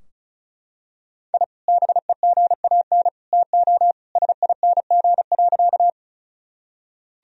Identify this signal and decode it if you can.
Morse code, heard as a keyed tone, about 34 wpm, 700 Hz. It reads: I BEGAN TO SING.
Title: Sentences from Top 1000 Words - Review Part 2 - 35wpm